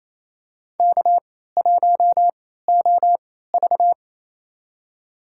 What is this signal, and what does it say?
Morse code, heard as a keyed tone, about 28 wpm, spaced as for 16 wpm, 700 Hz. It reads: K1OV
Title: USA Call Signs - 4 Characters - 3x Character Spacing - 28wpm